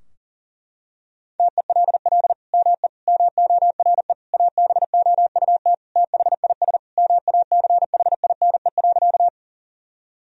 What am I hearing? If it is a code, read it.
Morse code, heard as a keyed tone, about 37 wpm, 700 Hz. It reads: TELL ME MORE ABOUT THIS MACHINE.
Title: Sentences from Top 1000 Words - Review Part 2 - 40wpm